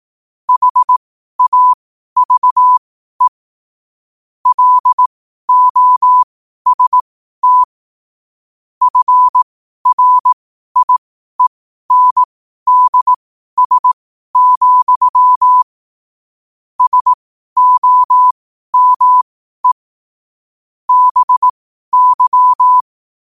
Morse code, heard as a keyed tone, about 18 wpm, spaced as for 12 wpm, 1000 Hz. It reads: HAVE LOST FRIENDS, SOME BY
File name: qotdfiles / 12wpm